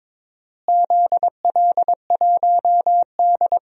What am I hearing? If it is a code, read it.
Morse code, heard as a keyed tone, about 21 wpm, 700 Hz. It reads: ZL1D
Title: DXCC-100 Call Signs - 4 Characters - Repeated 3x - 22wpm